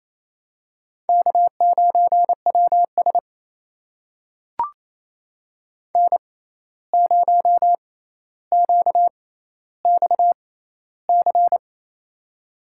Morse code, heard as a keyed tone, about 28 wpm, spaced as for 13 wpm, 700 Hz. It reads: K9WS N0QXC
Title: USA Call Signs - 6x Character Spacing - 28wpm